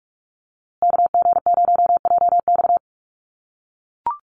Morse code, heard as a keyed tone, about 44 wpm, 690 Hz. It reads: KG0JX
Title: USA Call Signs Repeated 3x 45wpm